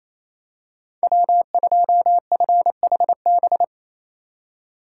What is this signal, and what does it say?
Morse code, heard as a keyed tone, about 27 wpm, 700 Hz. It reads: W2FHB
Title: USA Call Signs - 5 Characters - 2x Character Spacing - 28wpm